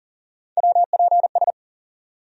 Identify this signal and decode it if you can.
Morse code, heard as a keyed tone, about 37 wpm, 700 Hz. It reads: WPS
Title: USA Call Sign Postfixes - 3 Letters 40wpm